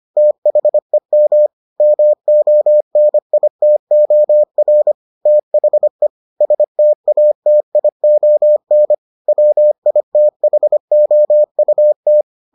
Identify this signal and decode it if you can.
Morse code, heard as a keyed tone, about 25 wpm, 600 Hz. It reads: THEM MONITOR THE STATION WITHOUT